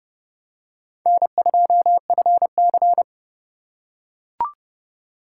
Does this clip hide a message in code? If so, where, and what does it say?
Yes: Morse code from 1.1 to 3.0 s, heard as a keyed tone, about 29 wpm, 700 Hz. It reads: N2FC